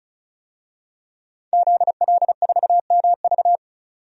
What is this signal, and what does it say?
Morse code, heard as a keyed tone, about 34 wpm, 700 Hz. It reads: ZL4MV
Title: DXCC-100 Call Signs - 5 Characters - 35wpm